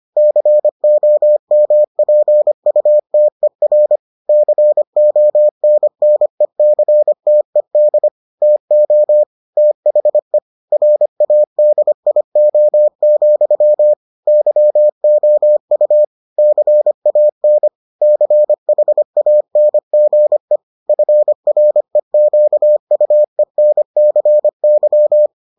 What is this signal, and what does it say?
Morse code, heard as a keyed tone, about 25 wpm, 600 Hz. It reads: COMPUTER CONNECTED TO THE RADIO, YOU CAN CHANGE FREQUENCY